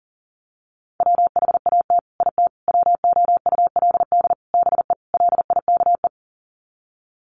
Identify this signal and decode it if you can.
Morse code, heard as a keyed tone, about 40 wpm, 700 Hz. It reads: WHAT IT WOULD BE LIKE
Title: Sets of 5 Words Encore 40wpm